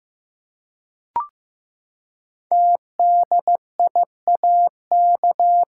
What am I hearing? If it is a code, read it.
Morse code, heard as a keyed tone, about 15 wpm, 700 Hz. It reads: TDIAK
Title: Code Groups - 5 Letters - 15wpm